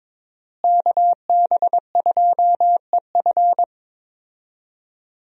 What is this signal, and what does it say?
Morse code, heard as a keyed tone, about 22 wpm, 700 Hz. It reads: KB2EF
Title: USA Call Signs - 5 Characters - Repeated 2x - 22wpm